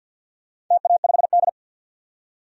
Morse code, heard as a keyed tone, about 47 wpm, 700 Hz. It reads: TA5D
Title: DXCC-100 Call Signs - 4 Characters - 50wpm